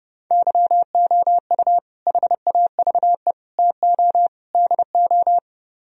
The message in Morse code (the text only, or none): YOU HAVE TO DO